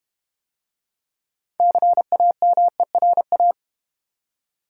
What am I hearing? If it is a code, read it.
Morse code, heard as a keyed tone, about 31 wpm, 700 Hz. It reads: CAMERA
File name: Easy Words - Speed-Racing with Hints - 25wpm